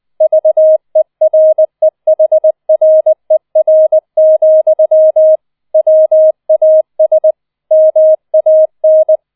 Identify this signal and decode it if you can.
Morse code, heard as a keyed tone, about 20 wpm, 630 Hz. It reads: VEREHRER, WAS MAN